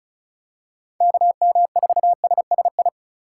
Morse code, heard as a keyed tone, about 36 wpm, 700 Hz. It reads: KM4SSI
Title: USA Call Signs - 6 Characters - 4x Character Spacing - 35wpm